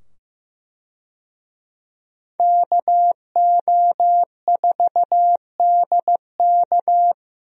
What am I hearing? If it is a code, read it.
Morse code, heard as a keyed tone, about 15 wpm, 700 Hz. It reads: KO4DK
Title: USA Call Signs - 5 Characters - 15wpm